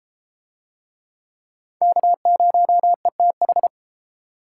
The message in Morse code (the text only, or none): K0ETH